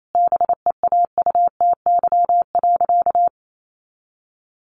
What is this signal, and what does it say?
Morse code, heard as a keyed tone, about 28 wpm, 700 Hz. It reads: BEAUTY.
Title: Sentences from Top 1400 Words 28wpm